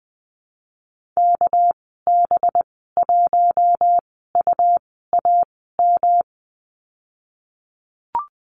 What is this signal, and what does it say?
Morse code, heard as a keyed tone, about 20 wpm, spaced as for 15 wpm, 700 Hz. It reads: KB1UAM
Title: 213 Lesson 34g - Number 8 - Call Signs 20wpm